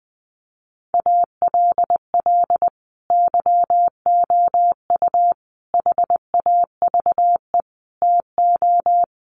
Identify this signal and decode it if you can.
Morse code, heard as a keyed tone, about 19 wpm, 700 Hz. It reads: ALL YOU HAVE TO
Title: Sets of 5 Words 20wpm